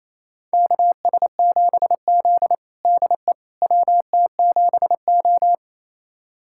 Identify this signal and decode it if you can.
Morse code, heard as a keyed tone, about 28 wpm, 700 Hz. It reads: KS7Z DE WT7O
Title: USA Callsign DE Callsign - 4 Characters - 28wpm